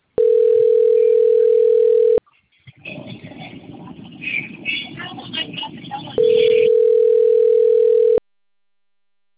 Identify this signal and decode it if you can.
Ringback tone (North America), 440 + 480 Hz.